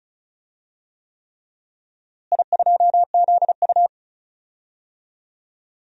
Morse code, heard as a keyed tone, about 36 wpm, 700 Hz. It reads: I2ZU
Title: DXCC-100 Call Signs - 4 Characters - Repeated 3x - 35wpm